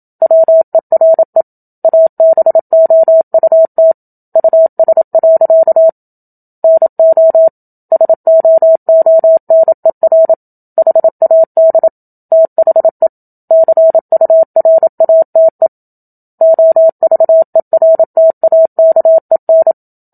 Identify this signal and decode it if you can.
Morse code, heard as a keyed tone, about 27 wpm, 650 Hz. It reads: WERE ABOUT US. NO SOONER HAD THE CURATE OVERTAKEN
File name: Book Two / 01-Chapter 1 - 27 WPM